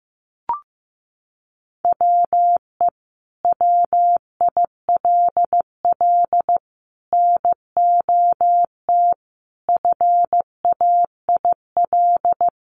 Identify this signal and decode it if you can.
Morse code, heard as a keyed tone, about 15 wpm, 700 Hz. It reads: WE WILL NOT FAIL
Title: Sentences from Top 1100 Words 15wpm